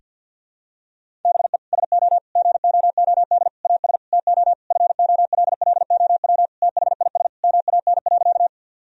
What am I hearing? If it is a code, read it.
Morse code, heard as a keyed tone, about 52 wpm, 700 Hz. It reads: BE SO GOOD AS TO FOLLOW THIS MAN.
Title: Sentences from Top 1000 Words - Review Part 1 - 50wpm